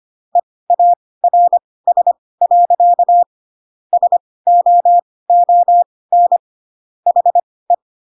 Morse code, heard as a keyed tone, about 25 wpm, spaced as for 18 wpm, 700 Hz. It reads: EARS. SOON HE